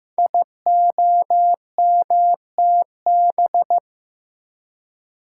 Morse code, heard as a keyed tone, about 15 wpm, 700 Hz. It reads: IOMTB
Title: Code Groups - 5 Letters - 15wpm